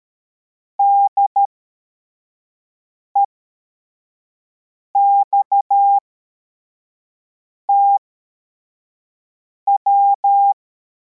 Morse code, heard as a keyed tone, about 13 wpm, spaced as for 4 wpm, 800 Hz. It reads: DEXTW
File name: formation / cw / Text81